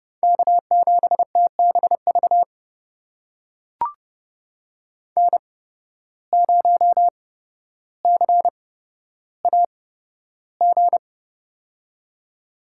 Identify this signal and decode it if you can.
Morse code, heard as a keyed tone, about 32 wpm, spaced as for 13 wpm, 700 Hz. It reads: K7TBV N0CAG
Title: USA Call Signs - 5 Characters - 8x Character Spacing - 30wpm